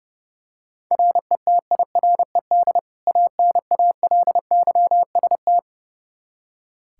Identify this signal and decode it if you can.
Morse code, heard as a keyed tone, about 31 wpm, 700 Hz. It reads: RETIRED ANALYST
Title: K1USN SST QSO Practice 30wpm